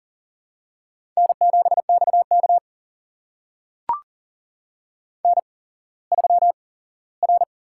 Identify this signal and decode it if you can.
Morse code, heard as a keyed tone, about 39 wpm, spaced as for 19 wpm, 700 Hz. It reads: N7XK N3R
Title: USA Call Signs - 4 Characters - 8x Character Spacing - 40wpm